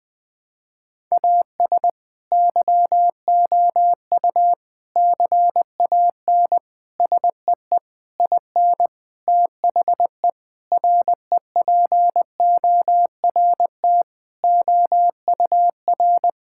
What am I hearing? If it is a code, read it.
Morse code, heard as a keyed tone, about 20 wpm, 700 Hz. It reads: AS YOU CAN SEE IN THE REPORT OUR
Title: Sentences with 6-Word Phrases - 20wpm